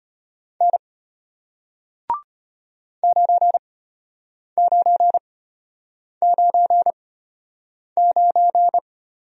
Morse code, heard as a keyed tone, about 34 wpm, spaced as for 8 wpm, 700 Hz. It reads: N 9999